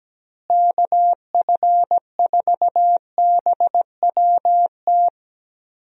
Morse code, heard as a keyed tone, about 17 wpm, 700 Hz. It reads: KF4BWT
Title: USA Call Signs - 6 Characters - Repeated 3x - 17wpm